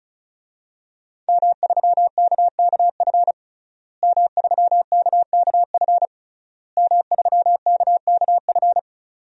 Morse code, heard as a keyed tone, about 36 wpm, 700 Hz. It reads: M3KKF M3KKF M3KKF
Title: DXCC-100 Call Signs - 5 Characters - Repeated 3x - 35wpm